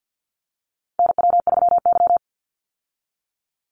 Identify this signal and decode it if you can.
Morse code, heard as a keyed tone, about 50 wpm, 700 Hz. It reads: NW3Y
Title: Call Signs - 50wpm